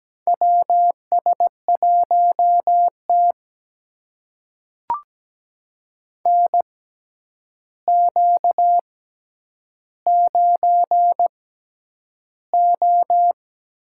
Morse code, heard as a keyed tone, about 17 wpm, spaced as for 9 wpm, 700 Hz. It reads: WS1T NQ9O
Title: USA Call Signs - 4 Characters - 6x Character Spacing - 17wpm